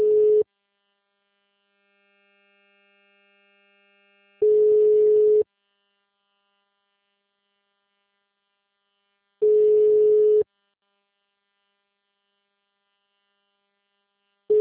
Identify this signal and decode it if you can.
Ringback tone (Europe, 425 Hz).